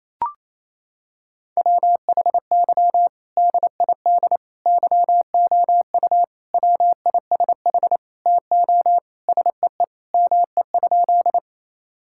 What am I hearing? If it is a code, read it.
Morse code, heard as a keyed tone, about 28 wpm, 700 Hz. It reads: WHY DID YOU WISH TO SEE ME?
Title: Sentences from Top 1600 Words 28wpm